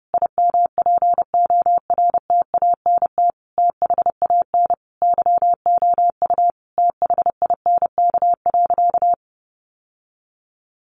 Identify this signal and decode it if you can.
Morse code, heard as a keyed tone, about 30 wpm, 700 Hz. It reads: IMPORTANT THAN YOU THINK.
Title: Sentences from Top 600 Words 30wpm